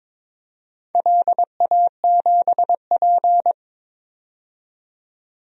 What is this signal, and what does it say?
Morse code, heard as a keyed tone, about 22 wpm, 700 Hz. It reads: LA7P
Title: DXCC-100 Call Signs - 4 Characters - 22wpm